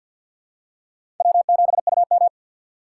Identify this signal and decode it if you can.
Morse code, heard as a keyed tone, about 53 wpm, 700 Hz. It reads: W7UM
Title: USA Call Signs - 4 Characters - 3x Character Spacing - 50wpm